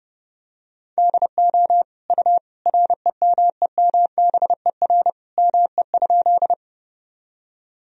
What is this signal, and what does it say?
Morse code, heard as a keyed tone, about 32 wpm, 700 Hz. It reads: DO U REMEMBER ME?
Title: Sentences from Top 400 Words and Abbreviations 30wpm